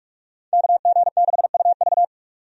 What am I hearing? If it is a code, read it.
Morse code, heard as a keyed tone, about 44 wpm, 700 Hz. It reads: KG6UV
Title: USA Call Signs - 5 Characters - 45wpm